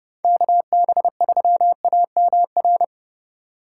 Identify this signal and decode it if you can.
Morse code, heard as a keyed tone, about 28 wpm, 700 Hz. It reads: KB3AMR